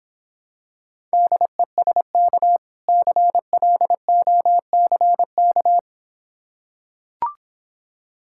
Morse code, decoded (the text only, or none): DESK CLOCK